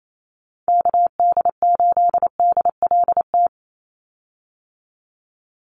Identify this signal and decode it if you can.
Morse code, heard as a keyed tone, about 28 wpm, 700 Hz. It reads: KD8DLT
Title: USA Call Signs 28wpm